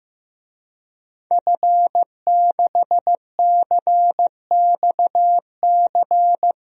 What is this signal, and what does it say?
Morse code, heard as a keyed tone, about 15 wpm, 700 Hz. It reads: F6CXC
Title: DXCC-100 Call Signs - 5 Characters - Repeated 2x - 15wpm